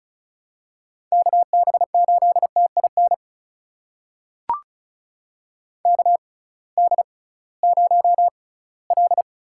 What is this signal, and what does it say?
Morse code, heard as a keyed tone, about 34 wpm, spaced as for 20 wpm, 700 Hz. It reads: KB8TIN KD0L
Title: USA Call Signs - 6 Characters - 6x Character Spacing - 35wpm